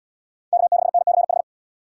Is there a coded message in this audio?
Yes: Morse code from 0.5 to 1.4 s, heard as a keyed tone, about 78 wpm, 700 Hz. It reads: K6NCH